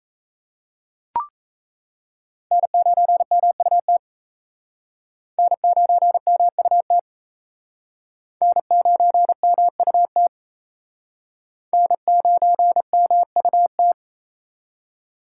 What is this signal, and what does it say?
Morse code, heard as a keyed tone, about 35 wpm, 700 Hz. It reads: N9MUT N9MUT N9MUT N9MUT
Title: Speed Racing - USA 1x3 Call Signs 28wpm